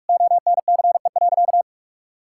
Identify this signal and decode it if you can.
Morse code, heard as a keyed tone, about 41 wpm, 700 Hz. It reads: ONCE.
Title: Sentences from Top 1000 Words - Review Part 5 - 45wpm